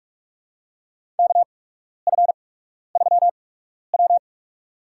Morse code, heard as a keyed tone, about 48 wpm, spaced as for 12 wpm, 700 Hz. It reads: KF3W
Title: USA Call Signs - 4 Characters - 8x Character Spacing - 45wpm